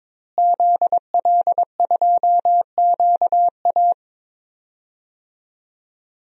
Morse code, heard as a keyed tone, about 22 wpm, 700 Hz. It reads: ZL2QA
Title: DXCC-100 Call Signs - 5 Characters - Repeated 2x - 22wpm